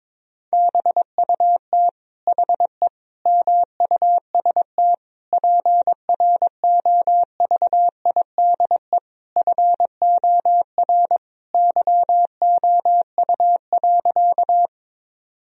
Morse code, heard as a keyed tone, about 21 wpm, 700 Hz. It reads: BUT HE MUST PROVIDE FOR YOU.
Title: Sentences from Top 1000 Words - Review Part 3 - 22wpm